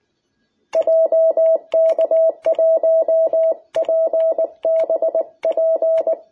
Morse code, heard as a keyed tone, about 20 wpm, 660 Hz. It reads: JK1PBP
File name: jh8cbh / 2021cw